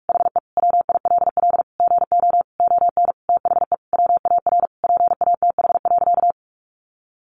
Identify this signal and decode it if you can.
Morse code, heard as a keyed tone, about 44 wpm, 700 Hz. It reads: HE WILL GO ON THE WAR PATH.
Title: Sentences from Top 1400 Words 45wpm